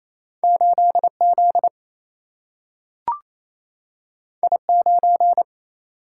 Morse code, heard as a keyed tone, about 28 wpm, 700 Hz. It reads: 8Z I9
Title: Single-Letter Single-Number 28wpm